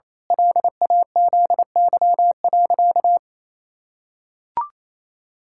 Morse code, heard as a keyed tone, about 28 wpm, 700 Hz. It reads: LAZY.
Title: Pangram Challenge v2 - 28wpm